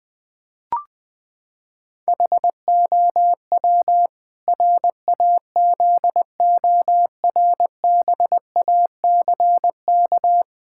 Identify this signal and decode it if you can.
Morse code, heard as a keyed tone, about 21 wpm, 700 Hz. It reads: HOW RAZORBACK
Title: Pangram Challenge v2 - 20wpm